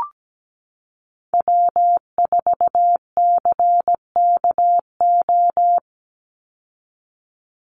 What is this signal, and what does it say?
Morse code, heard as a keyed tone, about 17 wpm, 700 Hz. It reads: W4CKO
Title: USA Call Signs 17wpm